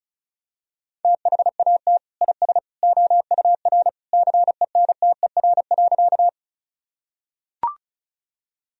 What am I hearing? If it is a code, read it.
Morse code, heard as a keyed tone, about 36 wpm, 700 Hz. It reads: THAT IS OUR CENTER.
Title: Sentences from Top 1000 Words - Review Part 2 - 35wpm